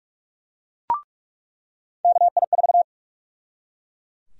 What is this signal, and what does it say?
Morse code, heard as a keyed tone, about 42 wpm, 700 Hz. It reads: KI4
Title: USA Call Sign Prefixes - 45wpm